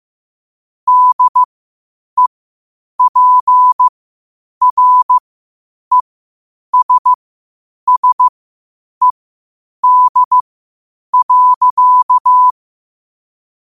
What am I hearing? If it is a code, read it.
Morse code, heard as a keyed tone, about 15 wpm, spaced as for 8 wpm, 1000 Hz. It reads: DEPRESSED.
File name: qotdfiles / 8wpm